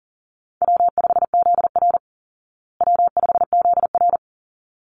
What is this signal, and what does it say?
Morse code, heard as a keyed tone, about 40 wpm, 700 Hz. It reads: W5ZR W5ZR
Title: Easy Call Signs Repeated 2x - 40wpm